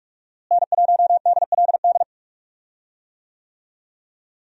Morse code, heard as a keyed tone, about 46 wpm, 700 Hz. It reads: N1DLD
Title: USA Call Signs - 5 Characters - Repeated 2x - 45wpm